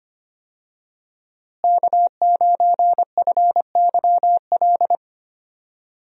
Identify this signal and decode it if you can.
Morse code, heard as a keyed tone, about 25 wpm, 700 Hz. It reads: K9FYL
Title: USA Call Signs - 5 Characters - Repeated 2x - 25wpm